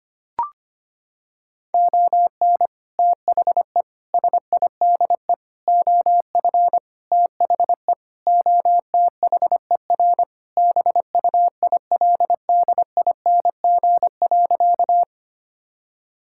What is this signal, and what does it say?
Morse code, heard as a keyed tone, about 25 wpm, 700 Hz. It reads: ON THE SIDE OF THE OTHER BUILDING.
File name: practice / 25wpm / Sentences with 3-Word Phrases - 25wpm